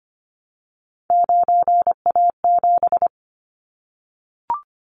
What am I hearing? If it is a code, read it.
Morse code, heard as a keyed tone, about 25 wpm, 700 Hz. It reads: 9A7
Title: Callsign Prefixes for Top 50 DXCC - 25wpm